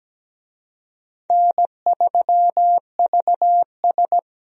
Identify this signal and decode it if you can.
Morse code, heard as a keyed tone, about 17 wpm, 700 Hz. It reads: N3VS
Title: USA Call Signs - 4 Characters - 17wpm